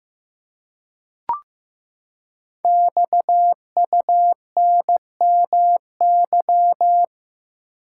Morse code, heard as a keyed tone, about 15 wpm, 700 Hz. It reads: XUNMY
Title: Code Groups - 5 Letters - 15wpm